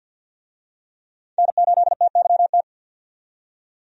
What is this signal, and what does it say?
Morse code, heard as a keyed tone, about 52 wpm, 700 Hz. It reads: N8TYT
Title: USA Call Signs - 5 Characters - 50wpm